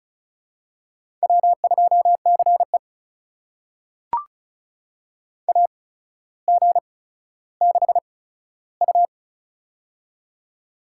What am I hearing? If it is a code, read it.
Morse code, heard as a keyed tone, about 36 wpm, spaced as for 15 wpm, 700 Hz. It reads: W2CE AG6U
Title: USA Call Signs - 4 Characters - 8x Character Spacing - 35wpm